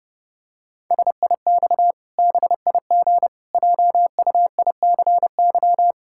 Morse code, heard as a keyed tone, about 28 wpm, 700 Hz. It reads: SIX BIG JUICY
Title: Pangram Challenge v2 - 30wpm